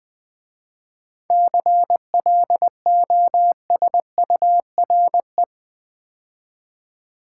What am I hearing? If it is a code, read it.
Morse code, heard as a keyed tone, about 20 wpm, 700 Hz. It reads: CLOSURE
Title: 7 Letter Words - First 3 Letters Given - 20wpm